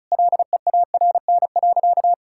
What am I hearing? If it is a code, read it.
Morse code, heard as a keyed tone, about 36 wpm, 700 Hz. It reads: LEARN.
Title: Sentences from Top 1000 Words - Review Part 6 - 35wpm